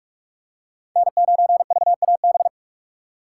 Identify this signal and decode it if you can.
Morse code, heard as a keyed tone, about 46 wpm, 700 Hz. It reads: N9VAB
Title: USA Call Signs - 5 Characters - 4x Character Spacing - 45wpm